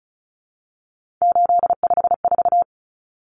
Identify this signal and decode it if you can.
Morse code, heard as a keyed tone, about 34 wpm, 700 Hz. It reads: 854
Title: Common Cut Numbers - 3-Digit 35wpm